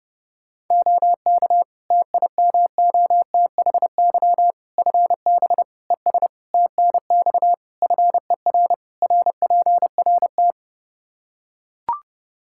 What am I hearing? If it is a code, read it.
Morse code, heard as a keyed tone, about 30 wpm, 700 Hz. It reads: OK TIMOTHY FB ES TNX FER RPRT
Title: K1USN SST QSO Practice 30wpm